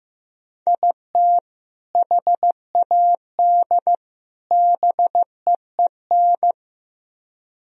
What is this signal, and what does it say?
Morse code, heard as a keyed tone, about 17 wpm, 700 Hz. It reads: IT HAD BEEN
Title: Speed-Racing - Sets of 3 Words 15wpm